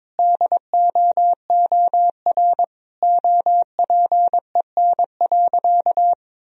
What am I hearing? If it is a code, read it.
Morse code, heard as a keyed tone, about 22 wpm, 700 Hz. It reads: DOOR OPEN.
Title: Sentences from Top 1000 Words - Review Part 3 - 22wpm